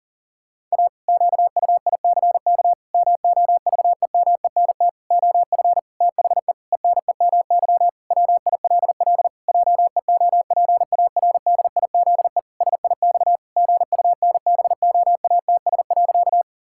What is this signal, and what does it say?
Morse code, heard as a keyed tone, about 40 wpm, 700 Hz. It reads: A QUICK MOVEMENT OF THE ENEMY WILL JEOPARDIZE SIX GUNBOATS.
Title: Pangram Challenge v2 - 40wpm